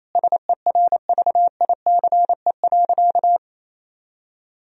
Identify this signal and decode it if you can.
Morse code, heard as a keyed tone, about 27 wpm, 700 Hz. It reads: SERVICE.
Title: Sentences from Top 1000 Words - Review Part 4 - 28wpm